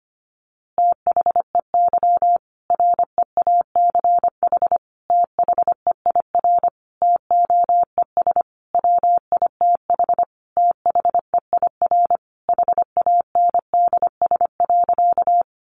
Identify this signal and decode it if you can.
Morse code, heard as a keyed tone, about 25 wpm, 700 Hz. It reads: THEY REACH THEIR TOES WITH THEIR HANDS.
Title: Easy Phrases 25wpm